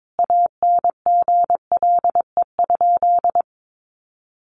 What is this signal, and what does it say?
Morse code, heard as a keyed tone, about 22 wpm, 700 Hz. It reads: ANGLE?
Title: Sentences from Top 1500 Words 22wpm